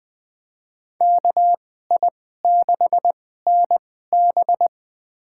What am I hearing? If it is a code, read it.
Morse code, heard as a keyed tone, about 20 wpm, spaced as for 14 wpm, 700 Hz. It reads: KI6NB
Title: USA Call Signs - 5 Characters - 2x Character Spacing - 20wpm